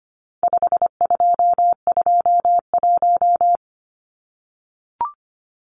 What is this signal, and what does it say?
Morse code, heard as a keyed tone, about 25 wpm, 700 Hz. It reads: 5221
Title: Numbers - 4-Digit 25wpm